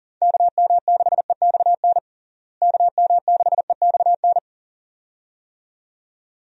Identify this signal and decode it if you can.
Morse code, heard as a keyed tone, about 37 wpm, 700 Hz. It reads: KM6EXN KM6EXN
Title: USA Call Signs - 6 Characters - Repeated 2x - 40wpm